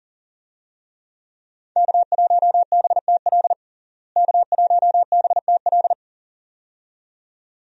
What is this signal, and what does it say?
Morse code, heard as a keyed tone, about 40 wpm, 700 Hz. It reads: K1BTL K1BTL
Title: USA Call Signs - 5 Characters - Repeated 2x - 40wpm